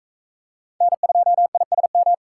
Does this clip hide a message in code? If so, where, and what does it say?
Yes: Morse code from 0.8 to 2.1 s, heard as a keyed tone, about 38 wpm, 700 Hz. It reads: N2ISM